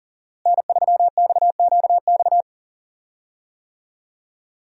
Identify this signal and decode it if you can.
Morse code, heard as a keyed tone, about 40 wpm, 700 Hz. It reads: N3XQX
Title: USA Call Signs - 5 Characters - Repeated 2x - 40wpm